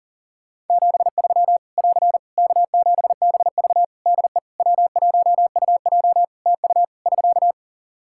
Morse code, heard as a keyed tone, about 40 wpm, 700 Hz. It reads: 73 <AR> K7BV DE W1UJ TU <SK>
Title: K1USN SST QSO Practice 40wpm